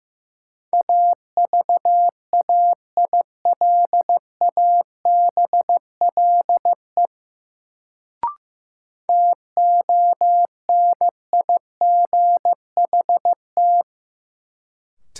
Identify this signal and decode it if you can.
Morse code, heard as a keyed tone, about 15 wpm, 700 Hz. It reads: AVAILABLE TONIGHT
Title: Sentences from Top 600 Words - New Word Drill - 15wpm